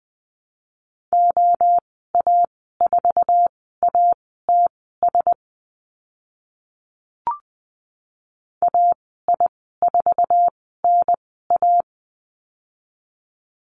Morse code, heard as a keyed tone, about 20 wpm, spaced as for 14 wpm, 700 Hz. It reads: OA4ATS AI4NA
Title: 069 Lesson 12g - Letter d - Call Signs 20wpm